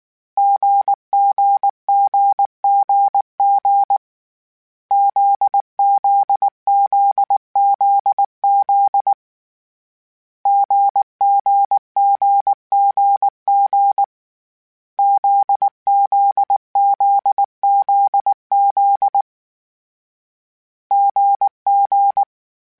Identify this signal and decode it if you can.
Morse code, heard as a keyed tone, about 19 wpm, 800 Hz. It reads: GGGGG ZZZZZ GGGGG ZZZZZ GG
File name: CW Code Groups-mp3s / Sets mp3s / GZ GZ7 sp @18